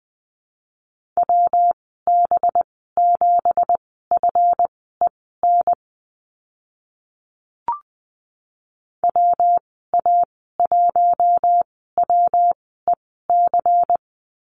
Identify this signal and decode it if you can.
Morse code, heard as a keyed tone, about 20 wpm, spaced as for 15 wpm, 700 Hz. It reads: WB7FEN WA1WEC
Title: 186 Lesson 30g - Letter b - Call Signs 20wpm